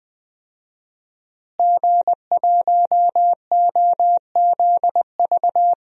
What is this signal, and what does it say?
Morse code, heard as a keyed tone, about 20 wpm, 700 Hz. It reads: G1OZV
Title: DXCC-100 Call Signs - 5 Characters - Repeated 2x - 20wpm